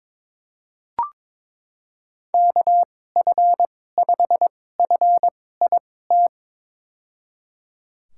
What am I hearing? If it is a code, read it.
Morse code, heard as a keyed tone, about 22 wpm, spaced as for 16 wpm, 700 Hz. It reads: KF5FIT
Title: USA Call Signs - 6 Characters - 2x Character Spacing - 22wpm